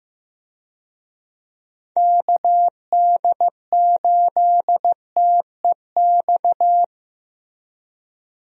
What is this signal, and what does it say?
Morse code, heard as a keyed tone, about 15 wpm, 700 Hz. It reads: KD8TEX
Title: USA Call Signs - 6 Characters - Repeated 3x - 15wpm